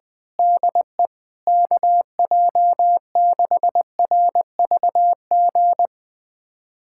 Morse code, heard as a keyed tone, about 21 wpm, 700 Hz. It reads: DE KJ6RVG
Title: USA Callsign DE Callsign - 6 Characters - 20wpm